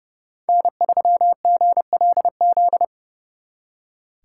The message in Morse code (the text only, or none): N3GLZ